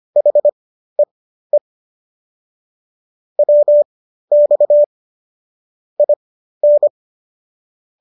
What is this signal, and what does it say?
Morse code, heard as a keyed tone, about 25 wpm, spaced as for 13 wpm, 590 Hz. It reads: HEE WX IN